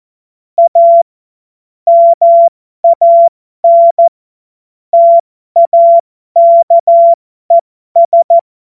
Morse code, heard as a keyed tone, about 14 wpm, spaced as for 12 wpm, 680 Hz. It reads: A MAN TAKES
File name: Morsekurs / Witze